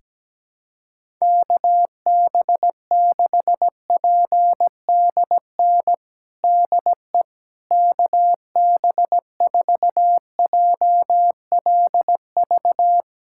Morse code, heard as a keyed tone, about 17 wpm, 700 Hz. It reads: KB6PDN DE KB4JLV
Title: USA Callsign DE Callsign - 6 Characters - 17wpm